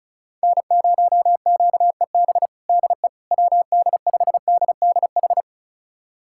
Morse code, heard as a keyed tone, about 36 wpm, 700 Hz. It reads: N0QEB DE WD5DDH
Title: USA Callsign DE Callsign 35wpm